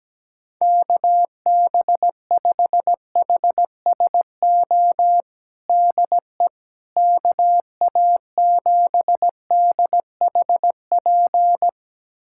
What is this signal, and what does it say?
Morse code, heard as a keyed tone, about 17 wpm, 700 Hz. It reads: KB5HSO DE KA7DHP
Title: USA Callsign DE Callsign 17wpm